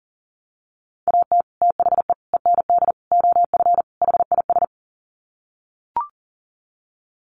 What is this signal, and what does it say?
Morse code, heard as a keyed tone, about 40 wpm, 700 Hz. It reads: AT THE END OF HIS
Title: Sets of 5 Words Encore 40wpm